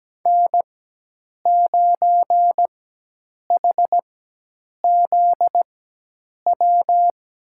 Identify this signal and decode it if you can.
Morse code, heard as a keyed tone, about 17 wpm, spaced as for 8 wpm, 700 Hz. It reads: N9HZW
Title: USA Call Signs - 5 Characters - 4x Character Spacing - 17wpm